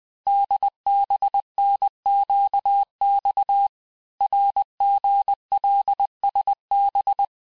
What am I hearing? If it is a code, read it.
Morse code, heard as a keyed tone, about 20 wpm, 800 Hz. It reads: DBNQX RGLSB
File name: cours-858-texte-chiffre-vitesse-19mpm